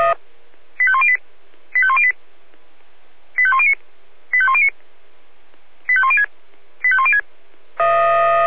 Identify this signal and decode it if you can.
Five-tone selective call (ZVEI), heard as quick runs of five tones, tones of about 70 ms. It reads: 85108 85108 85109 85109 85106 85106